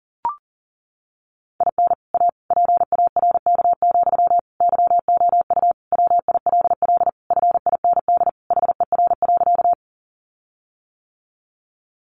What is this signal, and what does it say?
Morse code, heard as a keyed tone, about 40 wpm, 700 Hz. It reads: IN A PARK, YOU WILL FIND HER.
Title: Sentences from Top 1300 Words 40wpm